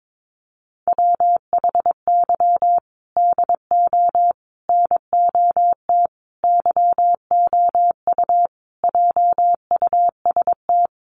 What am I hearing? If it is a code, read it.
Morse code, heard as a keyed tone, about 22 wpm, 700 Hz. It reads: WHY DO NOT YOU JUST
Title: Sets of 5 Words 22wpm